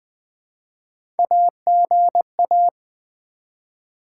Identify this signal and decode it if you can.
Morse code, heard as a keyed tone, about 20 wpm, 700 Hz. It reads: AGA
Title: Build A Word - 1 Letter At A Time - 5-Letter Words - 20wpm